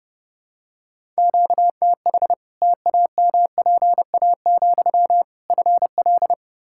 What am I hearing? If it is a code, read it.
Morse code, heard as a keyed tone, about 29 wpm, 700 Hz. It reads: QTH TAMPA, FL
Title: Practice QSO - Exchange 1 - Beginning - 30wpm